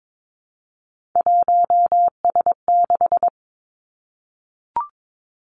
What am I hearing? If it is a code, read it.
Morse code, heard as a keyed tone, about 22 wpm, 700 Hz. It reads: 1S6
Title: Three Challenging Letter-Number Combinations 22wpm